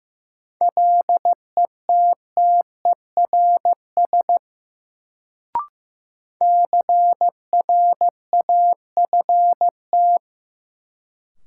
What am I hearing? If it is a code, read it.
Morse code, heard as a keyed tone, about 15 wpm, 700 Hz. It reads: LETTERS CRAFT
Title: Sentences from Top 1000 Words - Review Part 6 - New Word Drill - 15wpm